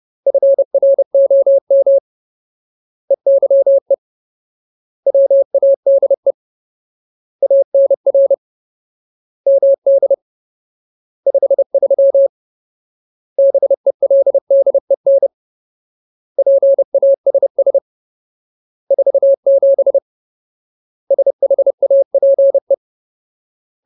Morse code, heard as a keyed tone, about 30 wpm, 550 Hz. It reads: FROM EYE WADE ANR MD 53 BELDEN PASS 47 SHAPE
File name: Ultimate Word List / 30WPM Ultimate